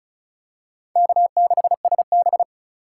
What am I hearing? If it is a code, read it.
Morse code, heard as a keyed tone, about 36 wpm, 700 Hz. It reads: K6SB
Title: USA Call Signs - 4 Characters - 3x Character Spacing - 35wpm